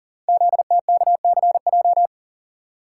Morse code, heard as a keyed tone, about 40 wpm, 700 Hz. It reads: ZTKCJ
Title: Code Groups - 5 Letters - 40wpm